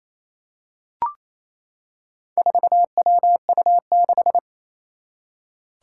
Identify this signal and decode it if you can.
Morse code, heard as a keyed tone, about 28 wpm, 700 Hz. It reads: 4WU6
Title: Four Challenging Letter-Number Combinations 28wpm